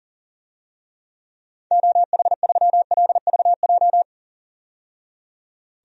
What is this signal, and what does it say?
Morse code, heard as a keyed tone, about 40 wpm, 700 Hz. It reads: OH3LVJ
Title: DXCC-100 Call Signs - 6 Characters - 40wpm